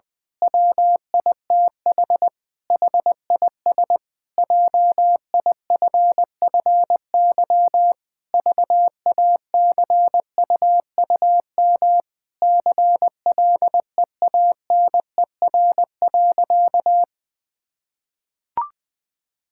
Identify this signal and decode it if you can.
Morse code, heard as a keyed tone, about 20 wpm, 700 Hz. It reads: WITH HIS JIFFY VACUUM CLEANER.
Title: Pangram Challenge v2 - 20wpm